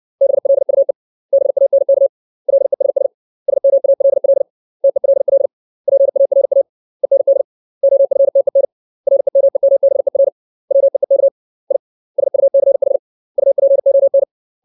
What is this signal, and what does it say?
Morse code, heard as a keyed tone, about 60 wpm, 550 Hz. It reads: BLUE BANK LESS SMALL TELL PARA END OFTEN REMEMBER WEEK I SUCH UPON
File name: 632 words / 632 Words 60WPM